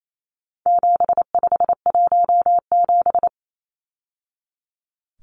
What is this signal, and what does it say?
Morse code, heard as a keyed tone, about 28 wpm, 700 Hz. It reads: 7517
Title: Numbers - 4-Digit 28wpm